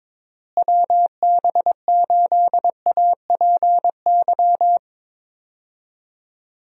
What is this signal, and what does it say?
Morse code, heard as a keyed tone, about 22 wpm, 700 Hz. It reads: WB8APY
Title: USA Call Signs - 6 Characters - Repeated 2x - 22wpm